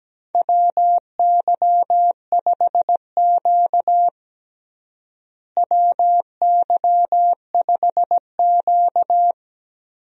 Morse code, heard as a keyed tone, about 17 wpm, 700 Hz. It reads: WY5Q WY5Q
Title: USA Call Signs - 4 Characters - Repeated 2x - 17wpm